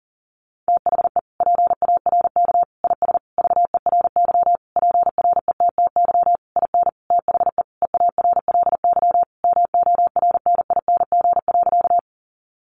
Morse code, heard as a keyed tone, about 40 wpm, 700 Hz. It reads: THE PARK IS VERY PRETTY IN THE EARLY MORNING.
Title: Sentences from Top 1300 Words 40wpm